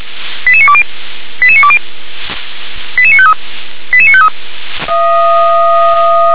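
Five-tone selective call (ZVEI), heard as quick runs of five tones, tones of about 70 ms. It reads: 88920 88920 88953 88953
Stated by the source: ZVEI five-tone selective call received by SDR.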